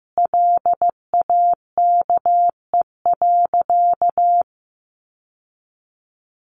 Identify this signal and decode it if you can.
Morse code, heard as a keyed tone, about 15 wpm, 700 Hz. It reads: LAKE.
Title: Sentences from Top 1100 Words 15wpm